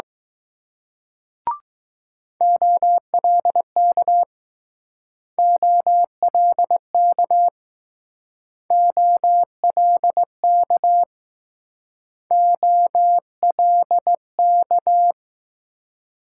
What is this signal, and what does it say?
Morse code, heard as a keyed tone, about 20 wpm, 700 Hz. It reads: OLK OLK OLK OLK